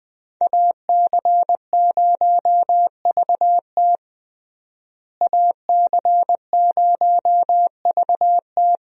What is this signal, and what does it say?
Morse code, heard as a keyed tone, about 20 wpm, 700 Hz. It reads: AC0VT AC0VT
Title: USA Call Signs - 5 Characters - Repeated 2x - 20wpm